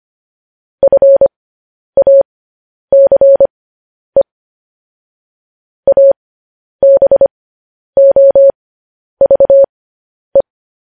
Morse code, heard as a keyed tone, about 25 wpm, spaced as for 10 wpm, 550 Hz. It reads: FACE ABOVE
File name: podcast / News headlines-1x-25x10